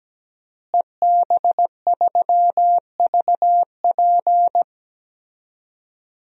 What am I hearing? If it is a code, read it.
Morse code, heard as a keyed tone, about 17 wpm, 700 Hz. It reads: EB3VP
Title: DXCC-100 Call Signs - 5 Characters - Repeated 3x - 17wpm